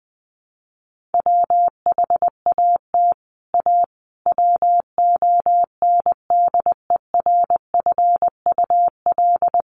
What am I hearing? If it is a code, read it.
Morse code, heard as a keyed tone, about 20 wpm, 700 Hz. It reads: WHAT A WONDERFUL
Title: Sentences from Top 1100 Words 20wpm